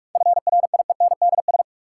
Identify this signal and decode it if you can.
Morse code, heard as a keyed tone, about 46 wpm, 700 Hz. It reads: FRIENDS
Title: Sentences from Top 1000 Words - Review Part 5 - New Word Drill - 45wpm